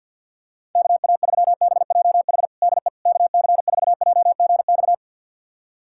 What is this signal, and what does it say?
Morse code, heard as a keyed tone, about 52 wpm, 700 Hz. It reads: KA3BJH DE KC4JGX
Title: USA Callsign DE Callsign - 6 Characters - 50wpm